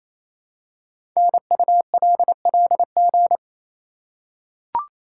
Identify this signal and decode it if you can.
Morse code, heard as a keyed tone, about 27 wpm, 700 Hz. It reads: NULLG
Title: Code Groups - 5 Letter:Numbers - 28wpm